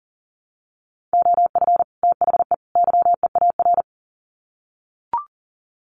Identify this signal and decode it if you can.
Morse code, heard as a keyed tone, about 40 wpm, 700 Hz. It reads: OF THE YEAR
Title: Sets of 3 Words 40wpm